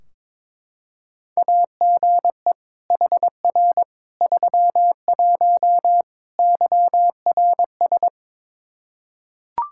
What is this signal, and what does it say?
Morse code, heard as a keyed tone, about 22 wpm, 700 Hz. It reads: AGE HR 31 YRS
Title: Practice QSO - Exchange 3 - More Info - 22wpm